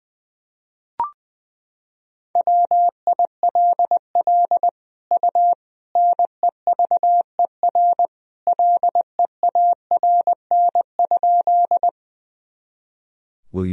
Morse code, heard as a keyed tone, about 20 wpm, 700 Hz. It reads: WILL U NEVER LEARN?
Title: Sentences from Top 300 Words and Abbreviations 20wpm